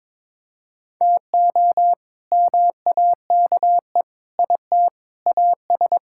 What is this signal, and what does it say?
Morse code, heard as a keyed tone, about 22 wpm, 700 Hz. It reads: TO MAKE IT AS
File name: practice / 22wpm / Sets of 8 Words - 22wpm